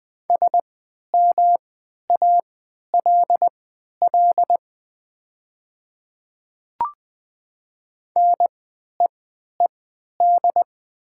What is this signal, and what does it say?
Morse code, heard as a keyed tone, about 20 wpm, spaced as for 11 wpm, 700 Hz. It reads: SMALL NEED
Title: 3x Character Spacing - Top 200 Words - 20wpm